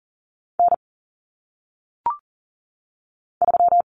Morse code, heard as a keyed tone, about 40 wpm, 700 Hz. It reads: N 3
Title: Common Cut Numbers - 1-Digit 40wpm